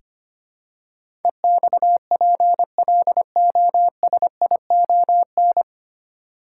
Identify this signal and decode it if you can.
Morse code, heard as a keyed tone, about 25 wpm, 700 Hz. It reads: EXPLOSION
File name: Nine Letter Words - 25wpm